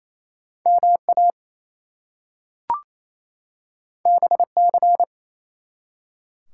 Morse code, heard as a keyed tone, about 28 wpm, 700 Hz. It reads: MA BC
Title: US States and Canadian Provinces 28wpm